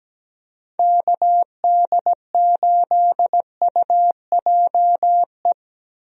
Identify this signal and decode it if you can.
Morse code, heard as a keyed tone, about 17 wpm, 700 Hz. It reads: KD8UJE
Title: USA Call Signs - 6 Characters - 17wpm